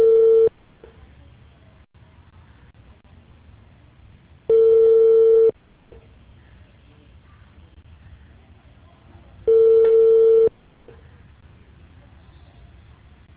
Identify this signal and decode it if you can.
Ringback tone (China, 450 Hz).